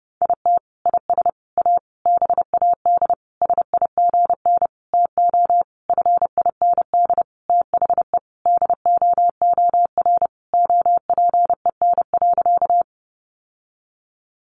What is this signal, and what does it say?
Morse code, heard as a keyed tone, about 30 wpm, 700 Hz. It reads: IT IS A BAD SIGN TO FIND THE DOOR OPEN.
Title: Sentences from Top 1000 Words 30wpm